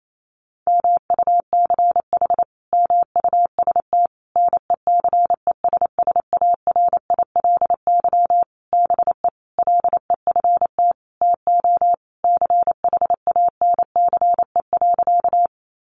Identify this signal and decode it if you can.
Morse code, heard as a keyed tone, about 28 wpm, 700 Hz. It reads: MUCH MUST NECESSARILY BE LEFT TO CHANCE.
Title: Sentences from Top 1400 Words 28wpm